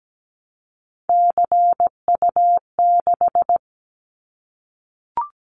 Morse code, heard as a keyed tone, about 17 wpm, 700 Hz. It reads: CU6
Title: Callsign Prefixes for Top 50 DXCC - 17wpm